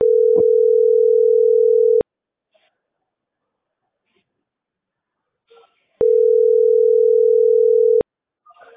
Ringback tone (North America), 440 + 480 Hz.